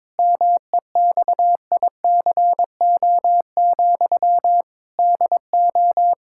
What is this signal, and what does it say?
Morse code, heard as a keyed tone, about 22 wpm, 700 Hz. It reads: MEXICO, DO
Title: Pangram Challenge v2 - 22wpm